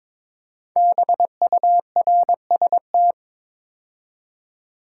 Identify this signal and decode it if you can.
Morse code, heard as a keyed tone, about 22 wpm, 700 Hz. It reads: BURST
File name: practice / 22wpm / 5 Letter Words - First 3 Letters Given - 22wpm